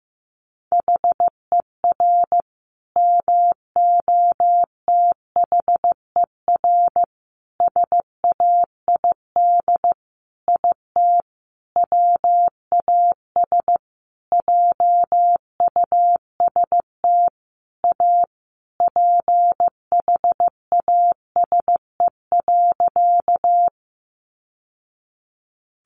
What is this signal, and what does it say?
Morse code, heard as a keyed tone, about 15 wpm, 700 Hz. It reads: HER MOTHER SAID IT WAS JUST A PHASE.
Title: Sentences from Top 1200 Words 15wpm